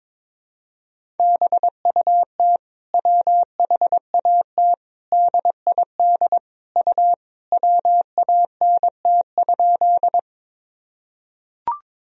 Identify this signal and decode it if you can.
Morse code, heard as a keyed tone, about 22 wpm, 700 Hz. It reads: BUT WHAT DID U WANT?
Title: Sentences from Top 200 Words and Abbreviations 22wpm